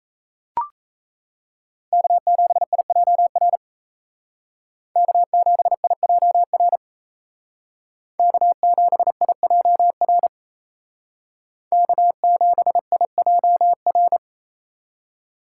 Morse code, heard as a keyed tone, about 34 wpm, 700 Hz. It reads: K7IJR K7IJR K7IJR K7IJR
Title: Speed Racing - USA 1x3 Call Signs 28wpm